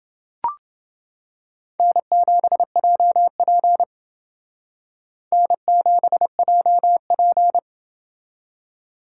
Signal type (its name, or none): Morse code